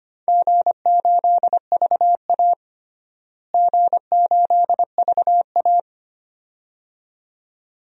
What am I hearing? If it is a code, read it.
Morse code, heard as a keyed tone, about 25 wpm, 700 Hz. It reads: G8VA G8VA
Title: DXCC-100 Call Signs - 4 Characters - Repeated 2x - 25wpm